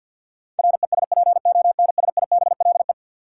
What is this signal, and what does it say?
Morse code, heard as a keyed tone, about 53 wpm, 700 Hz. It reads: RESPONSIBLE
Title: Sentences from Top 1000 Words - Review Part 7 - New Word Drill - 50wpm